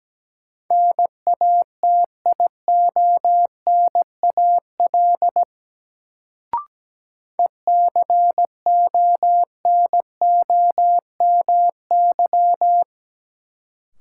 Morse code, heard as a keyed tone, about 17 wpm, 700 Hz. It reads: NATIONAL ECONOMY
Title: Sentences from Top 1200 Words - New Word Drill Alt - 17wpm